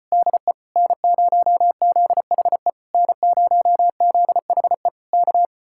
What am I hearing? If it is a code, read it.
Morse code, heard as a keyed tone, about 33 wpm, 700 Hz. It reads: DE N0ZHE N0ZHE K